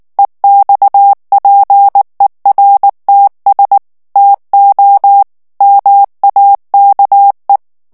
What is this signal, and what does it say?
Morse code, heard as a keyed tone, about 19 wpm, 800 Hz. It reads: EXPERTS TO MAKE
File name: ANGLAIS texte clair 2 vitesse 18 mots minute